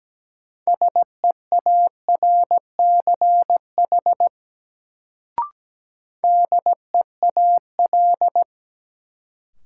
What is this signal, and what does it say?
Morse code, heard as a keyed tone, about 17 wpm, 700 Hz. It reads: SEARCH DEAL
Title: Sentences from Top 800 Words - New Word Drill - 17wpm